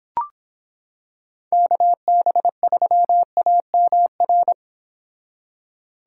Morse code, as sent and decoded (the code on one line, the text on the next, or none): -.- -... ...-- .- -- .-.
KB3AMR